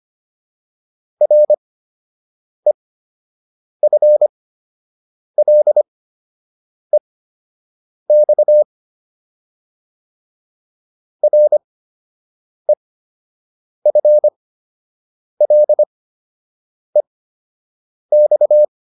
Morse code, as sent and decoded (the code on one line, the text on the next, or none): .-. ..-. .-.. -..- / .-. ..-. .-.. -..-
REFLEX REFLEX